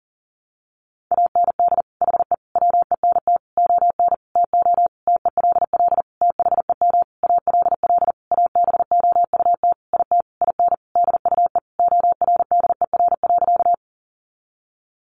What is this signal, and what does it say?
Morse code, heard as a keyed tone, about 40 wpm, 700 Hz. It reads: AND HE WENT ON TO TELL THEM ALL ABOUT IT IN DUE ORDER.
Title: Sentences from Top 900 Words 40wpm